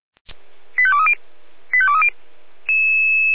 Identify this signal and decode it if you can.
Five-tone selective call (ZVEI), heard as quick runs of five tones, tones of about 70 ms. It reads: 85230 85230